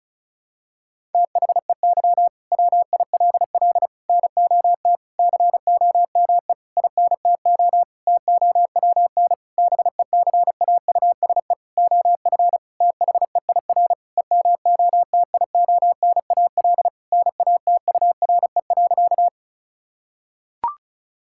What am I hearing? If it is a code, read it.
Morse code, heard as a keyed tone, about 34 wpm, 700 Hz. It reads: THEY WILL NOT COME INTO TOWN BECAUSE OF THEIR EMOTIONAL NATURE.
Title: Sentences from Top 1000 Words - Review Part 5 - 35wpm